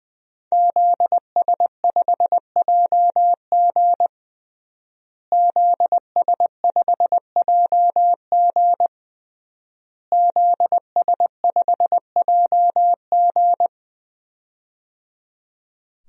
Morse code, heard as a keyed tone, about 20 wpm, 700 Hz. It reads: ZS5JG ZS5JG ZS5JG
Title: DXCC-100 Call Signs - 5 Characters - Repeated 3x - 20wpm